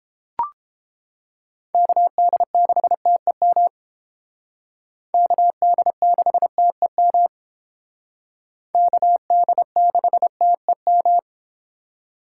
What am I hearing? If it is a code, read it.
Morse code, heard as a keyed tone, about 30 wpm, 700 Hz. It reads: KD6TEM KD6TEM KD6TEM